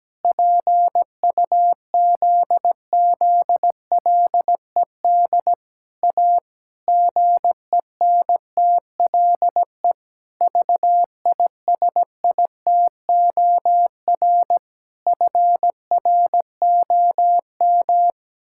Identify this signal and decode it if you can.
Morse code, heard as a keyed tone, about 17 wpm, 700 Hz. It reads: PUZZLED A GENTLE VISITOR FROM
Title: Pangram Challenge v2 - 17wpm